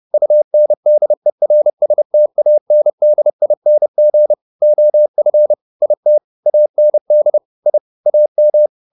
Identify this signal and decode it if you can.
Morse code, heard as a keyed tone, about 30 wpm, 600 Hz. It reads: UNDERSTANDING OF IT AND I AM